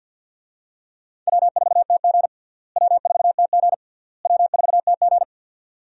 Morse code, heard as a keyed tone, about 53 wpm, 700 Hz. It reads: W4TG W4TG W4TG
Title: USA Call Signs - 4 Characters - Repeated 3x - 50wpm